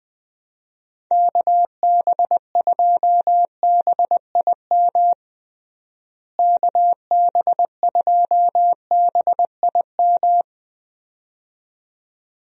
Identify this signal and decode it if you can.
Morse code, heard as a keyed tone, about 20 wpm, 700 Hz. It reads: KB2BIM KB2BIM
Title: USA Call Signs - 6 Characters - Repeated 2x - 20wpm